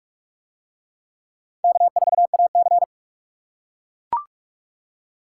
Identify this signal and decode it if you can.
Morse code, heard as a keyed tone, about 42 wpm, 700 Hz. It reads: K4AC